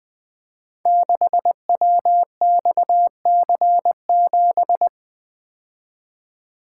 Morse code, heard as a keyed tone, about 21 wpm, 700 Hz. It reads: 6WXC7
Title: Code Groups - 5 Letter:Numbers - 20wpm